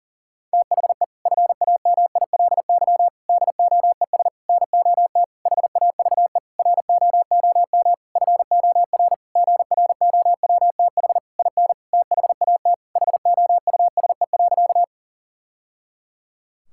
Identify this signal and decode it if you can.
Morse code, heard as a keyed tone, about 40 wpm, 700 Hz. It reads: THE FAMILY DOES NOT HAVE ROOM FOR GROWTH IN THAT HOUSE.
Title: Sentences from Top 1000 Words - Review Part 5 - 40wpm